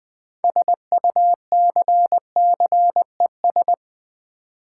Morse code, heard as a keyed tone, about 20 wpm, 700 Hz. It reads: SUCCES
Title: Build A Word - 6 Letters At A Time - 12-Letter Words - 20wpm